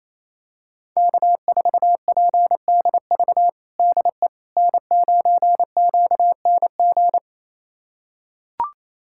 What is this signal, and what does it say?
Morse code, heard as a keyed tone, about 28 wpm, 700 Hz. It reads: K4PDV DE N9QNG
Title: USA Callsign DE Callsign 28wpm